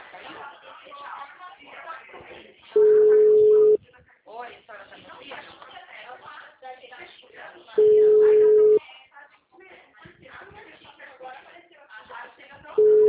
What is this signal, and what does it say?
Ringback tone (Europe, 425 Hz), with speech.